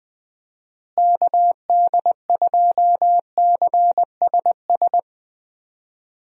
Morse code, heard as a keyed tone, about 20 wpm, 700 Hz. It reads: KD2CSS
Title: USA Call Signs - 6 Characters - Repeated 2x - 20wpm